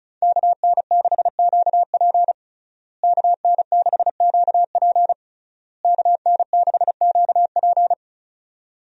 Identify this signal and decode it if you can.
Morse code, heard as a keyed tone, about 34 wpm, 700 Hz. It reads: KN6QP KN6QP KN6QP
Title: USA Call Signs - 5 Characters - Repeated 3x - 35wpm